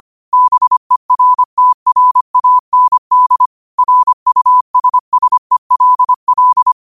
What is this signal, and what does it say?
Morse code, heard as a keyed tone, about 25 wpm, 1000 Hz. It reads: BERTRAND RUSSELL
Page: New quotes every day in morse code at 25 Words per minute.